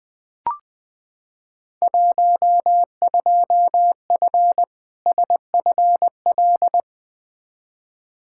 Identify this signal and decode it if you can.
Morse code, heard as a keyed tone, about 20 wpm, 700 Hz. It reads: 12F SFL
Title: Field Day Exchanges Encore - 20wpm